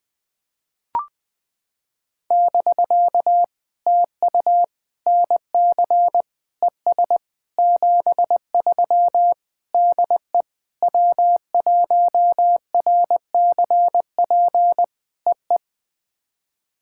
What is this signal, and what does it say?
Morse code, heard as a keyed tone, about 21 wpm, 700 Hz. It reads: <BK> TU NC ES 73 DE W1RCP EE
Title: POTA QSO Practice - 20wpm